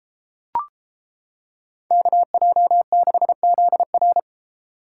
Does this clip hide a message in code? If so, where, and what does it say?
Yes: Morse code from 1.9 to 4.2 s, heard as a keyed tone, about 31 wpm, 700 Hz. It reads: KJ6ZR